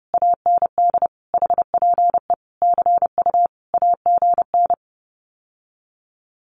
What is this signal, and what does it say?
Morse code, heard as a keyed tone, about 30 wpm, 700 Hz. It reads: AND HPE CU AGN
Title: Common QSO Phrases Encore 30wpm